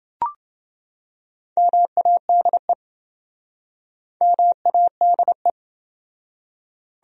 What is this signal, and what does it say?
Morse code, heard as a keyed tone, about 28 wpm, 700 Hz. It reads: MADE MADE